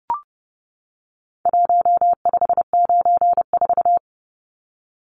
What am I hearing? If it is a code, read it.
Morse code, heard as a keyed tone, about 30 wpm, 700 Hz. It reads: 1594
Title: Numbers - Contest Style - 4-Digit 30wpm